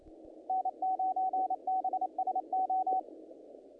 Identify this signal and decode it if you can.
Morse code, heard as a keyed tone, about 29 wpm, 750 Hz, with radio noise. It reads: N9BSO
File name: contests / naqp-cw-k5tr-op-k5pi- -2000